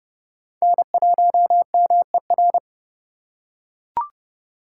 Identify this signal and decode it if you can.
Morse code, heard as a keyed tone, about 26 wpm, 700 Hz. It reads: N1MER